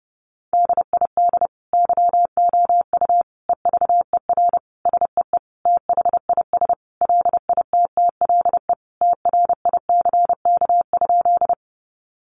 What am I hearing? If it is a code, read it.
Morse code, heard as a keyed tone, about 30 wpm, 700 Hz. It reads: DID YOU EVER SEE THIS LITTLE TRICK?
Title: Sentences from Top 1500 Words 30wpm